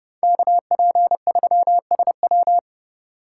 Morse code, heard as a keyed tone, about 39 wpm, 700 Hz. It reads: KP3SW